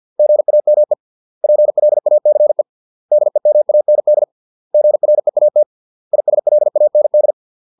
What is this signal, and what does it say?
Morse code, heard as a keyed tone, about 50 wpm, 600 Hz. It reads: GAME PLACE DEMAND GREAT ISLAND